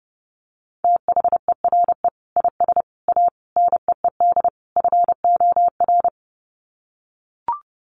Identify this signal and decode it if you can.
Morse code, heard as a keyed tone, about 30 wpm, 700 Hz. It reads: THERE IS A NEED FOR
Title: Sets of 5 Words Encore2 30wpm